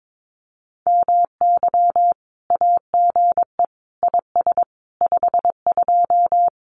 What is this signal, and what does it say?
Morse code, heard as a keyed tone, about 22 wpm, 700 Hz. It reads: MY AGE IS 52
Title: Common QSO Phrases Encore 22wpm